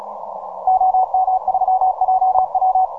Morse code, heard as a keyed tone, about 41 wpm, 750 Hz. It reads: OM3JW
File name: simple-file-list / VP8STI 15MCW